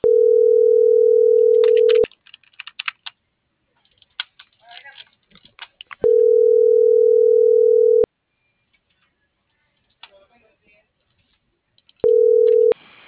Ringback tone (North America), 440 + 480 Hz.